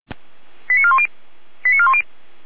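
Five-tone selective call (ZVEI), heard as quick runs of five tones, tones of about 70 ms. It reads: 89410 89410
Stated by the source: ZVEI five-tone selective call received on an SDR.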